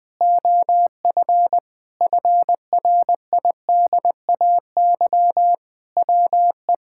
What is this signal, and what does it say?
Morse code, heard as a keyed tone, about 20 wpm, 700 Hz. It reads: OF FRIDAY WE
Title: Sentences with 6-Word Phrases - Spoken Beginning - 20wpm